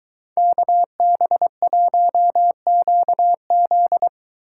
Morse code, heard as a keyed tone, about 23 wpm, 700 Hz. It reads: KB1QZ